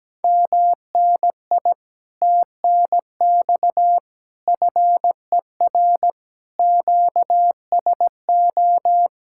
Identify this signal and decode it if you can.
Morse code, heard as a keyed tone, about 17 wpm, 700 Hz. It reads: MNI TNX FER QSO
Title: Speed-Racing Common QSO Phrases - 17wpm